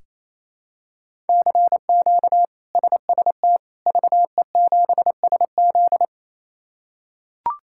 Morse code, heard as a keyed tone, about 27 wpm, 700 Hz. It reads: CQ SST VE7SZ
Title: K1USN SST QSO Practice - 28wpm